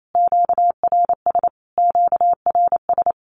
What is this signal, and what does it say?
Morse code, heard as a keyed tone, about 28 wpm, 700 Hz. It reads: QRS QRS
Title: Q-Codes Repeated 3x 28wpm